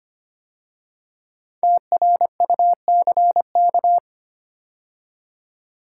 Morse code, heard as a keyed tone, about 25 wpm, 700 Hz. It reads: TRUCK
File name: 5 Letter Words - First 3 Letters Given - 25wpm